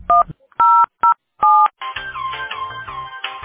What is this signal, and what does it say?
DTMF, heard as telephone touch tones, tones of about 210 ms. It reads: 1007